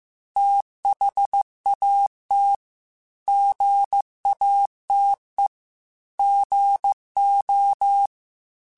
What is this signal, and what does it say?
Morse code, heard as a keyed tone, about 15 wpm, 800 Hz. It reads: THAT GATE GO
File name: Technical Subjects / Technical Subjects-mp3s / OR Gate @13